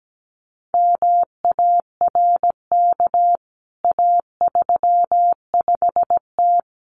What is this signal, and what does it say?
Morse code, heard as a keyed tone, about 17 wpm, 700 Hz. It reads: MARK A35T
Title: CWT Exchange 17wpm